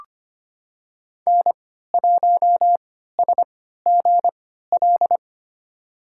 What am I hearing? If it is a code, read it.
Morse code, heard as a keyed tone, about 25 wpm, spaced as for 14 wpm, 700 Hz. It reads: N1SGL
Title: USA Call Signs - 5 Characters - 3x Character Spacing - 25wpm